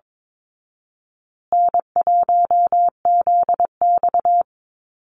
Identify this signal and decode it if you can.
Morse code, heard as a keyed tone, about 22 wpm, 700 Hz. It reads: N1ZX
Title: Easy Call Signs Repeated 3x - 22wpm